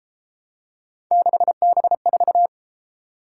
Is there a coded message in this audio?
Yes: Morse code from 1.1 to 2.5 s, heard as a keyed tone, about 31 wpm, 700 Hz. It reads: 6B4